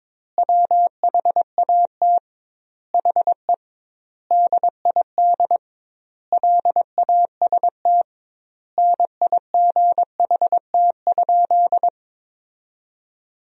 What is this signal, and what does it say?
Morse code, heard as a keyed tone, about 22 wpm, 700 Hz. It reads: WHAT HE DID LAST NIGHT?
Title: 2x Word-Spacing - Sentences from Top 300 Words - 22wpm